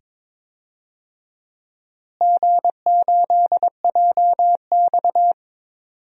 Morse code, heard as a keyed tone, about 22 wpm, 700 Hz. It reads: G8JX
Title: DXCC-100 Call Signs - 4 Characters - Repeated 2x - 22wpm